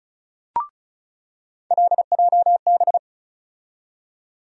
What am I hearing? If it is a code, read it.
Morse code, heard as a keyed tone, about 36 wpm, 700 Hz. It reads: LJB
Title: USA Call Sign Postfixes - 3 Letters 35wpm